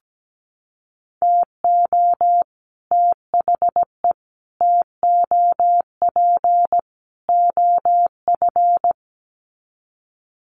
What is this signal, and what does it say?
Morse code, heard as a keyed tone, about 17 wpm, 700 Hz. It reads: TO THE TOP OF
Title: Sets of 4 Words 17wpm